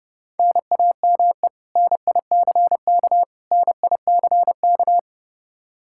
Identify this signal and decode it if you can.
Morse code, heard as a keyed tone, about 30 wpm, 700 Hz. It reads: NAME NICK NICK
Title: Practice QSO - Exchange 1 - Beginning - 30wpm